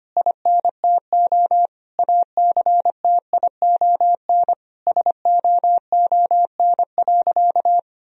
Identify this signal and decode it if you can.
Morse code, heard as a keyed tone, about 25 wpm, 700 Hz. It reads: INTO ACTION SOON.
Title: Sentences from Top 1000 Words - Review Part 4 - 25wpm